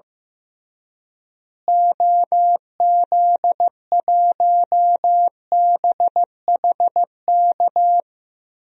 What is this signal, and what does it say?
Morse code, heard as a keyed tone, about 15 wpm, 700 Hz. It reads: OZ1BHK
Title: DXCC-100 Call Signs - 6 Characters - Repeated 3x - 15wpm